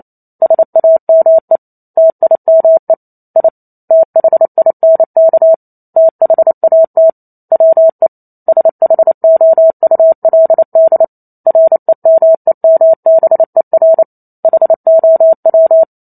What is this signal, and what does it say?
Morse code, heard as a keyed tone, about 28 wpm, 650 Hz. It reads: SAME TIME I THINK THAT WE SHOULD REMEMBER HOW